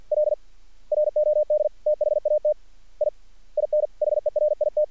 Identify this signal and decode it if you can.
Morse code, heard as a keyed tone, about 50 wpm, 600 Hz. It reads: L ROD THAT I INHERIT
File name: archived / CRSnet